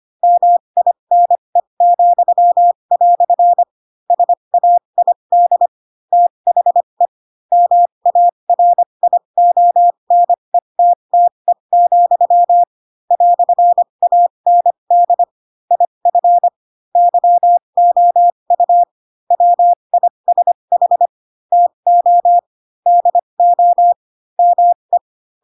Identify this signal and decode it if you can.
Morse code, heard as a keyed tone, about 25 wpm, spaced as for 22 wpm, 700 Hz. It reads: MINE," SAID THE MARIONETTE, "AND IF YOU WISH TO DO ME
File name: Pinocchio / Pinocchio ch4 18 25 ik1hge